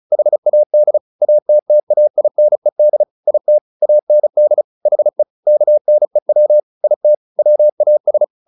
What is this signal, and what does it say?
Morse code, heard as a keyed tone, about 34 wpm, 600 Hz. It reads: HAD ATTAINED IT AND HE KNEW IT WAS
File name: the old man and the sea / the old man and the sea 1 35wpm